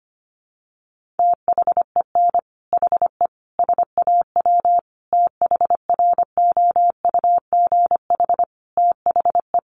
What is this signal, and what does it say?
Morse code, heard as a keyed tone, about 25 wpm, 700 Hz. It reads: THEN HE SAW THROUGH THE
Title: Sentences from Top 400 Words 25wpm